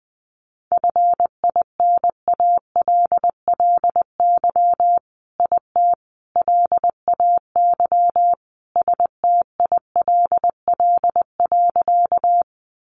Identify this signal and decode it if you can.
Morse code, heard as a keyed tone, about 20 wpm, 700 Hz. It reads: FINALLY IT LAY STILL.
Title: Sentences from Top 1000 Words 20wpm